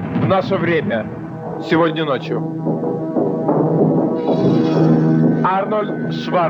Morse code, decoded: none

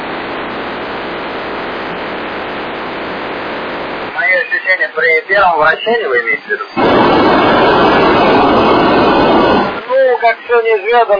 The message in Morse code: none

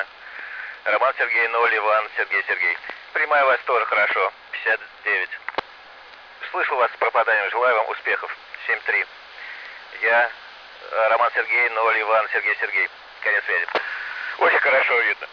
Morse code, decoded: none

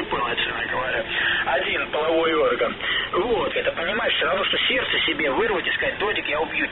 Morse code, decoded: none

